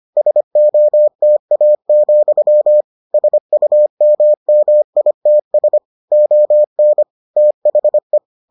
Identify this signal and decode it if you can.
Morse code, heard as a keyed tone, about 25 wpm, 600 Hz. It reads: SOTA, SUMMITS ON THE